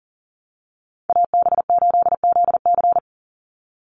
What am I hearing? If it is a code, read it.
Morse code, heard as a keyed tone, about 40 wpm, 700 Hz. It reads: AB8ZC
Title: USA Call Signs 40wpm